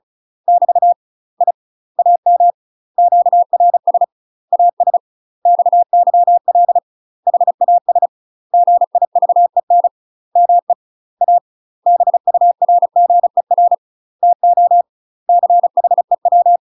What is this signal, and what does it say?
Morse code, heard as a keyed tone, about 34 wpm, 700 Hz. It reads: <BT> I AM QRS AS XYL HAS GIVEN ME A BURGER TO CHEW
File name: UK MASTER QSOs TRAINING / 35WPM / UK MASTER QSOs 0030